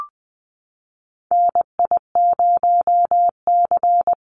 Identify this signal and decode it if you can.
Morse code, heard as a keyed tone, about 20 wpm, 700 Hz. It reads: NI0C
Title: Call Signs Repeated 2x - 20wpm